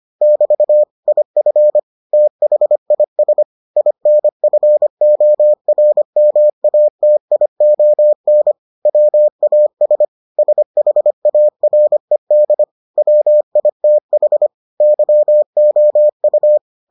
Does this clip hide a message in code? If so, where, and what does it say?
Yes: Morse code from 0.2 to 16.6 s, heard as a keyed tone, about 25 wpm, 600 Hz. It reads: <BT> IF THIS INFORMATION WAS SHARED WITH YOU